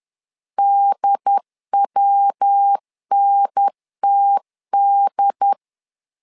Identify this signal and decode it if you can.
Morse code, heard as a keyed tone, about 10 wpm, 800 Hz. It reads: DWNTD
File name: LEZIONI / 3 MWRO / 3 MW 10wpm 1959